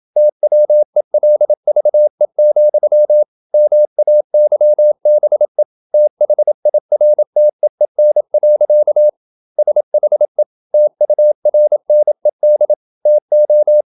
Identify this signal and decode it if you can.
Morse code, heard as a keyed tone, about 27 wpm, 600 Hz. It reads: TWELVE, MAYBE THIRTEEN. SHE TURNED TO
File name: Never 0027wpm